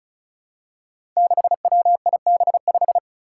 Morse code, heard as a keyed tone, about 34 wpm, 700 Hz. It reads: 6WIB5
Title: Five Challenging Letter-Number Combinations 35wpm